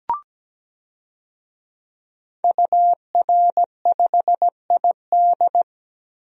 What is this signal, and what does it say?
Morse code, heard as a keyed tone, about 17 wpm, 700 Hz. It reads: UR5ID
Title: DXCC-100 Call Signs - 5 Characters - Repeated 3x - 17wpm